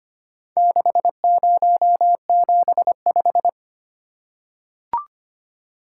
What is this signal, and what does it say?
Morse code, heard as a keyed tone, about 25 wpm, 700 Hz. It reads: 6075
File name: practice / 25wpm / Four Letter-Number Combinations - 25wpm